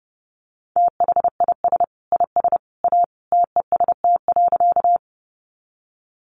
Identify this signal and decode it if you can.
Morse code, heard as a keyed tone, about 30 wpm, 700 Hz. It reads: THIS IS A TEST.
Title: Sentences from Top 500 Words 30wpm